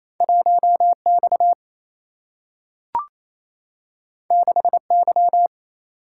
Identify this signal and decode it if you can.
Morse code, heard as a keyed tone, about 28 wpm, 700 Hz. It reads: 1X 6Y
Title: Single-Letter Single-Number 28wpm